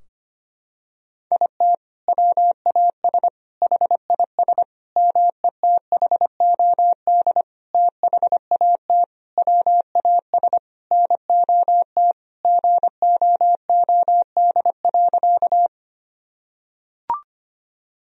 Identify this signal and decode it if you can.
Morse code, heard as a keyed tone, about 25 wpm, 700 Hz. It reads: IT WAS HIS METHOD THAT WAS NOT GOOD.
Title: Sentences from Top 1000 Words - Review Part 3 - 25wpm